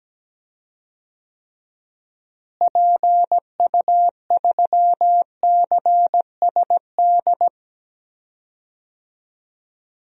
Morse code, heard as a keyed tone, about 17 wpm, 700 Hz. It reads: PU3CSD
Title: DXCC-100 Call Signs - 6 Characters - Repeated 2x - 17wpm